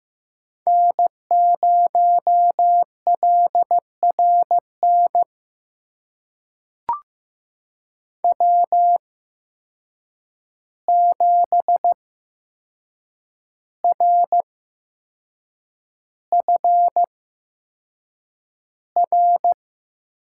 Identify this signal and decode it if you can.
Morse code, heard as a keyed tone, about 15 wpm, spaced as for 6 wpm, 700 Hz. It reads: N0LRN W7RFR
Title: USA Call Signs - 5 Characters - 8x Character Spacing - 15wpm